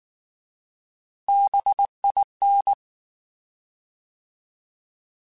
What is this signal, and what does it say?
Morse code, heard as a keyed tone, about 19 wpm, 800 Hz. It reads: BIN
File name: Alphabetical / Alphabetical mp3s / Bs @18